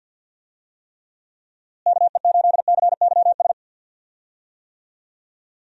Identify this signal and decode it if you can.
Morse code, heard as a keyed tone, about 49 wpm, 700 Hz. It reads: KE8CYS
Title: USA Call Signs - 6 Characters - Repeated 2x - 50wpm